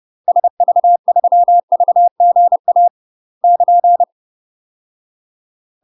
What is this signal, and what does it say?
Morse code, heard as a keyed tone, about 30 wpm, 700 Hz. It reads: SV3VGA <KN>
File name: UK MASTER QSOs TRAINING / 30WPM / UK MASTER QSOs 0015